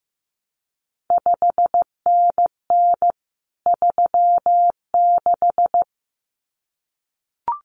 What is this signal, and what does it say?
Morse code, heard as a keyed tone, about 15 wpm, 700 Hz. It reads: 5NN 36
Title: CQ WW DX Contest Exchange Encore - 15wpm